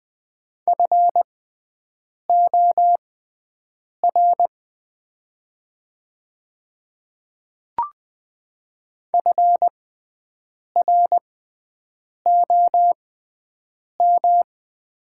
Morse code, heard as a keyed tone, about 20 wpm, spaced as for 7 wpm, 700 Hz. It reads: FOR FROM
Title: 6x Character Spacing - Top 100 Words - 20wpm